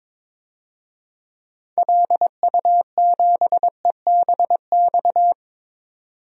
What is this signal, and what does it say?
Morse code, heard as a keyed tone, about 22 wpm, 700 Hz. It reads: LU7EBX
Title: DXCC-100 Call Signs - 6 Characters - 22wpm